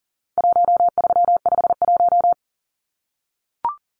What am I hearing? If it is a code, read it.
Morse code, heard as a keyed tone, about 40 wpm, 700 Hz. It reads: 1351
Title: Numbers - 4-Digit 40wpm